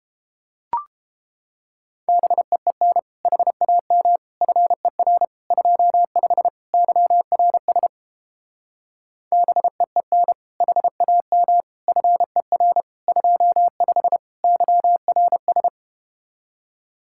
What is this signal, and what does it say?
Morse code, heard as a keyed tone, about 30 wpm, 700 Hz. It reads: BEEN HAM FER 25 YRS BEEN HAM FER 25 YRS